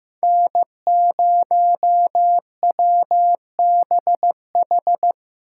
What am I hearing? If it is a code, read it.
Morse code, heard as a keyed tone, about 18 wpm, 700 Hz. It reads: N0WBH
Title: Speed Racing - USA 1x3 Call Signs 15wpm